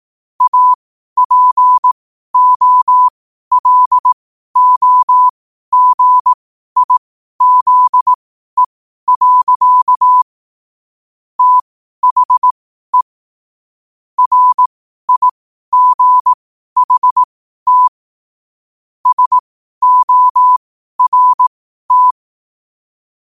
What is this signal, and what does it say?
Morse code, heard as a keyed tone, about 18 wpm, spaced as for 12 wpm, 1000 Hz. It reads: APOLOGIZE. THE RIGHT SORT
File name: qotdfiles / 12wpm